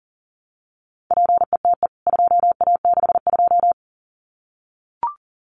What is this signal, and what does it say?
Morse code, heard as a keyed tone, about 40 wpm, 700 Hz. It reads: PETE 2A62
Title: CWT Exchange 40wpm